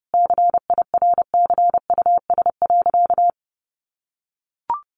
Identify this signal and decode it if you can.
Morse code, heard as a keyed tone, about 30 wpm, 700 Hz. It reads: CIRCUS.
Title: Easy Phrases 30wpm